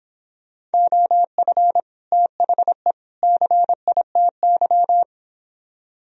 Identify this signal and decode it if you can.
Morse code, heard as a keyed tone, about 29 wpm, 700 Hz. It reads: OF THE CITY